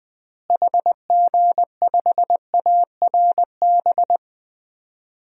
Morse code, heard as a keyed tone, about 20 wpm, 700 Hz. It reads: HG5ARB
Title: DXCC-100 Call Signs - 6 Characters - Repeated 2x - 20wpm